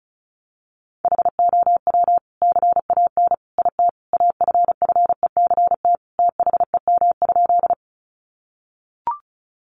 Morse code, heard as a keyed tone, about 34 wpm, 700 Hz. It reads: HOW CAN IT AFFECT THEM?
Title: Sentences from Top 900 Words 35wpm